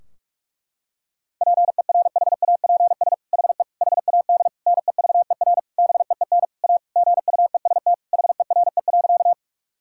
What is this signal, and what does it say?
Morse code, heard as a keyed tone, about 46 wpm, 700 Hz. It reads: PERHAPS HE HAD NEVER BEEN A GUEST HERE.
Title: Sentences from Top 1600 Words 45wpm